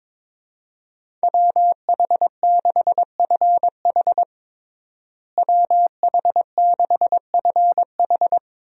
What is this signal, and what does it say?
Morse code, heard as a keyed tone, about 22 wpm, 700 Hz. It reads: WH6FH WH6FH
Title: USA Call Signs - 5 Characters - Repeated 3x - 22wpm